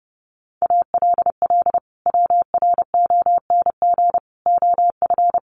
Morse code, heard as a keyed tone, about 28 wpm, 700 Hz. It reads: ALL WRONG OF
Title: Sentences from Top 900 Words 30wpm